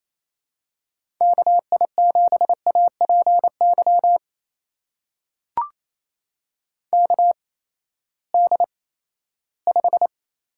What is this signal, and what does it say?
Morse code, heard as a keyed tone, about 27 wpm, spaced as for 16 wpm, 700 Hz. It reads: KI7APY KD5
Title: USA Call Signs - 6 Characters - 8x Character Spacing - 28wpm